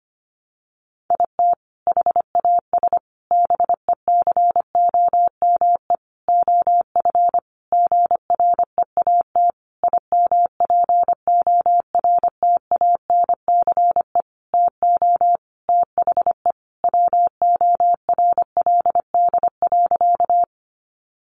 Morse code, heard as a keyed tone, about 25 wpm, 700 Hz. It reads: IT HAS BECOME OF GREAT IMPORTANCE TO THE WORLD.
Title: Sentences from Top 1200 Words 25wpm